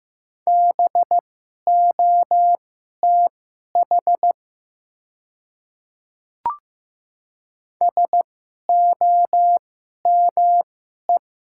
Morse code, heard as a keyed tone, about 15 wpm, spaced as for 11 wpm, 690 Hz. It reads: BOTH SOME
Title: 2x Character Spacing - Top 300 Words - 15wpm